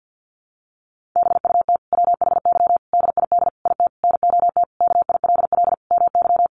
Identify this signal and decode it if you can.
Morse code, heard as a keyed tone, about 50 wpm, 700 Hz. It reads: BUT WHY DID IT NOT KILL MY
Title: Sentences from Top 1400 Words 50wpm